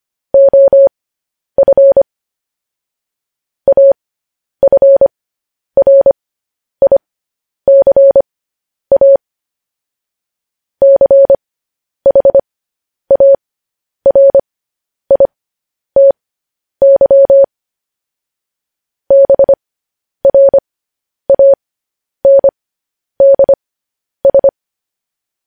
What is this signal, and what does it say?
Morse code, heard as a keyed tone, about 25 wpm, spaced as for 10 wpm, 550 Hz. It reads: OF AFRICA CHARITY BRANDS
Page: news-headlines-in-morse-code-at-10-wpm